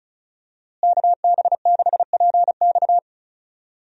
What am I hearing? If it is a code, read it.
Morse code, heard as a keyed tone, about 36 wpm, 700 Hz. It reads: KB6PX
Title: USA Call Signs - 5 Characters - Repeated 3x - 35wpm